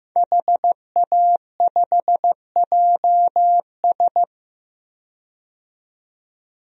Morse code, heard as a keyed tone, about 15 wpm, 700 Hz. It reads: HA5JS
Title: DXCC-100 Call Signs - 5 Characters - 15wpm